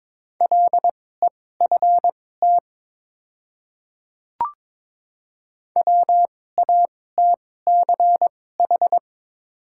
Morse code, heard as a keyed tone, about 22 wpm, spaced as for 16 wpm, 700 Hz. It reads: LEFT WATCH
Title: 2x Character Spacing - Top 300 Words - 22wpm